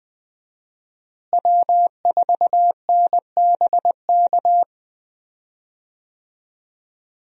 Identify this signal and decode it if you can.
Morse code, heard as a keyed tone, about 20 wpm, 700 Hz. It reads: W4NBK
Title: USA Call Signs - 5 Characters - Repeated 2x - 20wpm